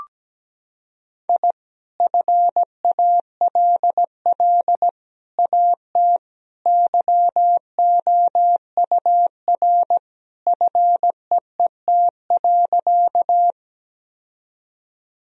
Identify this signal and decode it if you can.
Morse code, heard as a keyed tone, about 17 wpm, 700 Hz. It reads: I FALL AT YOUR FEET.
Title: Sentences from Top 1000 Words - Review Part 2 - 17wpm